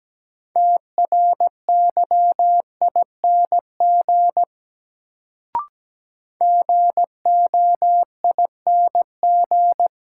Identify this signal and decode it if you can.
Morse code, heard as a keyed tone, about 17 wpm, 700 Hz. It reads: TRYING GOING
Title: Sets of 2 Words - New Word Drill - 17wpm